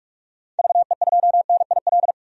Morse code, heard as a keyed tone, about 47 wpm, 700 Hz. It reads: VE2NIL
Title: DXCC-100 Call Signs - 6 Characters - 45wpm